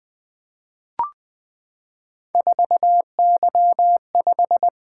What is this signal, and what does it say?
Morse code, heard as a keyed tone, about 20 wpm, 700 Hz. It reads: 4Y5
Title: Three Letter-Number Combinations - 20wpm